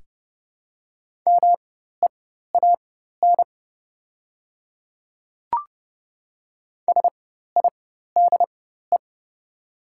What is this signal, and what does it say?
Morse code, heard as a keyed tone, about 30 wpm, spaced as for 14 wpm, 700 Hz. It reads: MEAN SIDE
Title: 4x Character Spacing - Top 200 Words - 30wpm